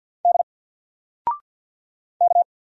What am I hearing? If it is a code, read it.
Morse code, heard as a keyed tone, about 50 wpm, 700 Hz. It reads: D K
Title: Single Letter-Number Alt 50wpm